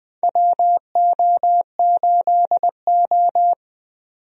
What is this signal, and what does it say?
Morse code, heard as a keyed tone, about 20 wpm, 700 Hz. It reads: WO8O
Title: USA Call Signs - 4 Characters - 20wpm